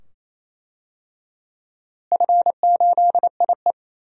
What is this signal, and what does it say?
Morse code, heard as a keyed tone, about 28 wpm, 700 Hz. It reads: F8IE
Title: DXCC-100 Call Signs - 4 Characters - Repeated 2x - 28wpm